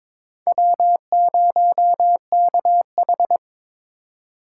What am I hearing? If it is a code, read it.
Morse code, heard as a keyed tone, about 22 wpm, 700 Hz. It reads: W0KH
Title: USA Call Signs - 4 Characters - 4x Character Spacing - 22wpm